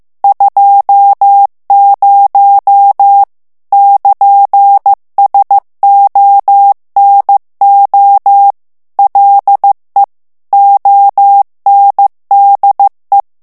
Morse code, heard as a keyed tone, about 15 wpm, 800 Hz. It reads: 20 <KN>SONO LE ONDE
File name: ITALIEN texte clair 1 vitesse 14 mots minute